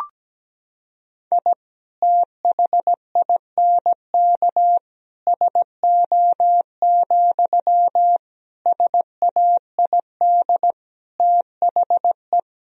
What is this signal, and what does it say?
Morse code, heard as a keyed tone, about 17 wpm, 700 Hz. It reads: I THINK SO, SAID THE
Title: Sentences from Top 1000 Words - Review Part 3 - 17wpm